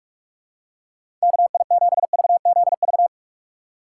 Morse code, heard as a keyed tone, about 44 wpm, 700 Hz. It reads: KI7VZV
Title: USA Call Signs - 6 Characters - 6x Character Spacing - 45wpm